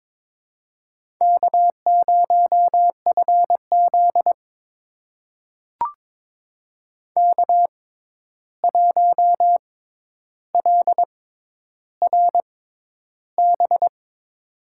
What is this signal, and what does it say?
Morse code, heard as a keyed tone, about 22 wpm, spaced as for 10 wpm, 700 Hz. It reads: K0FZ K1LRB
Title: USA Call Signs - 6x Character Spacing - 22wpm